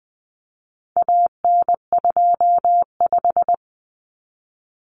Morse code, heard as a keyed tone, about 20 wpm, 700 Hz. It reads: AN25
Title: Common Cut Numbers - 4-Digit 20wpm